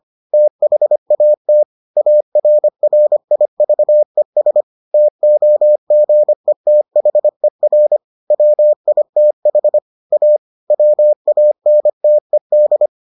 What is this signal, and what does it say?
Morse code, heard as a keyed tone, about 25 wpm, 600 Hz. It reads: THAT ARRIVES TOGETHER WITH A WANTED